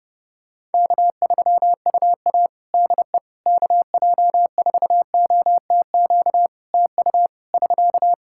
Morse code, heard as a keyed tone, about 30 wpm, 700 Hz. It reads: K3UA DE KJ4OTQ TU <SK>
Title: Practice QSO - Exchange 4 - Ending - 30wpm